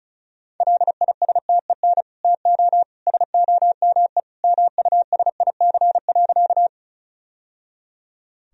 Morse code, heard as a keyed tone, about 34 wpm, 700 Hz. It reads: LISTEN TO SOME MUSIC.
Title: Sentences from Top 1000 Words - Review Part 1 - 35wpm